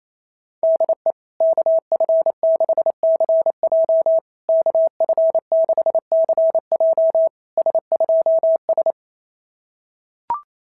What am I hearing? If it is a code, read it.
Morse code, heard as a keyed tone, about 27 wpm, 660 Hz. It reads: DE KF6CJ KF6CJ S2S
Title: SOTA QSO Practice 28wpm